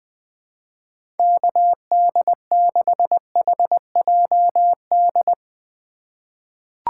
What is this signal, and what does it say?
Morse code, heard as a keyed tone, about 21 wpm, 700 Hz. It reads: KD6HJD
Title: USA Call Signs - 6 Characters - 4x Character Spacing - 20wpm